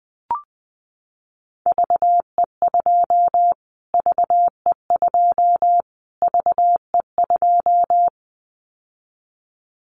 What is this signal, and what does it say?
Morse code, heard as a keyed tone, about 20 wpm, 700 Hz. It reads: VE2 VE2 VE2
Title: Callsign Prefixes for Top 25 DXCC - Repeated 3x - 20wpm